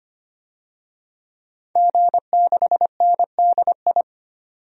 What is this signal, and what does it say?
Morse code, heard as a keyed tone, about 25 wpm, 700 Hz. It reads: G6NDI
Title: DXCC-100 Call Signs - 5 Characters - 25wpm